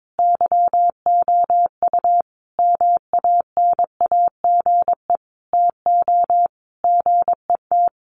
Morse code, heard as a keyed tone, about 22 wpm, 700 Hz. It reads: YOU MANAGE TO GET
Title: Sentences from Top 1400 Words 22wpm